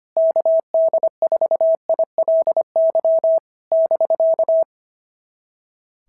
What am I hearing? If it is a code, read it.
Morse code, heard as a keyed tone, about 22 wpm, 650 Hz. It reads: KD4ILY <BK>
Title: SOTA QSO Practice 25wpm